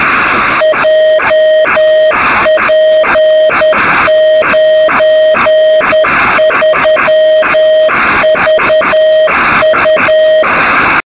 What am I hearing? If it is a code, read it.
Morse code, heard as a keyed tone, about 10 wpm, 630 Hz, with noise. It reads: JP93VU
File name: sm7lcb / 2203z sk2shf-b 23cm